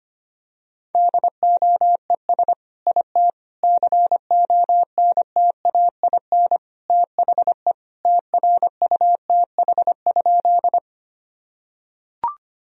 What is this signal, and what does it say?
Morse code, heard as a keyed tone, about 25 wpm, 700 Hz. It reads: DOES IT CONTAIN THE TRUTH?
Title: Sentences from Top 1000 Words - Review Part 2 - 25wpm